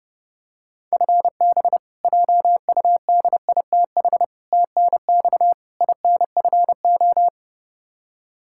Morse code, heard as a keyed tone, about 30 wpm, 700 Hz. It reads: FB JUDITH TNX INFO
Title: K1USN SST QSO Practice 30wpm